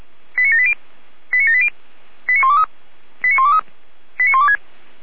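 Five-tone selective call (ZVEI), heard as quick runs of five tones, tones of about 80 ms. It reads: 89780 89780 89123 89123 89126
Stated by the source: ZVEI five-tone call received by SDR, station feuerwehr-orlamuende (Orlamünde fire brigade).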